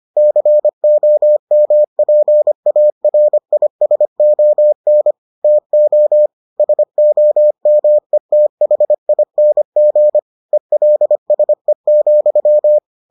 Morse code, heard as a keyed tone, about 25 wpm, 600 Hz. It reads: COMPARISON TO SOMETHING ELSE,